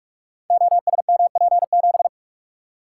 Morse code, heard as a keyed tone, about 44 wpm, 700 Hz. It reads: OSMP7
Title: Code Groups - 5 Letter:Numbers - 45wpm